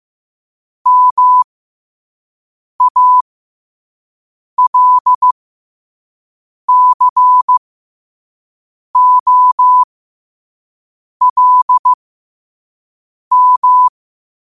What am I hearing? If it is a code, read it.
Morse code, heard as a keyed tone, about 15 wpm, spaced as for 5 wpm, 1000 Hz. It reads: MALCOLM
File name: qotdfiles / 5wpm